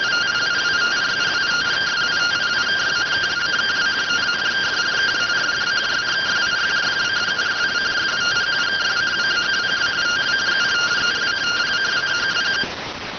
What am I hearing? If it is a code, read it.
Two-tone teleprinter signal, about 170 Hz shift.